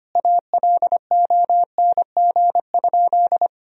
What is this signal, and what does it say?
Morse code, heard as a keyed tone, about 25 wpm, 700 Hz. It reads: ALONG?
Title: Sentences from Top 1000 Words - Review Part 3 - 25wpm